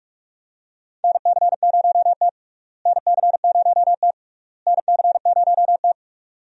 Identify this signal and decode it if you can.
Morse code, heard as a keyed tone, about 46 wpm, 700 Hz. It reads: NC0T NC0T NC0T
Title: USA Call Signs - 4 Characters - Repeated 3x - 45wpm